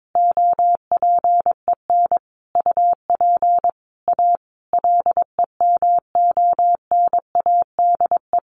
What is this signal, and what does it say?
Morse code, heard as a keyed tone, about 22 wpm, 700 Hz. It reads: OPEN UP A LEMONADE
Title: Additional Easy Phrases 22wpm